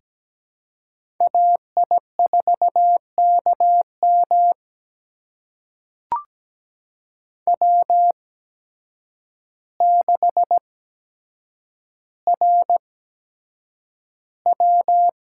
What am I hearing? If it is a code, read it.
Morse code, heard as a keyed tone, about 17 wpm, spaced as for 8 wpm, 700 Hz. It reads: AI4KM W6RW
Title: USA Call Signs - 8x Character Spacing - 17wpm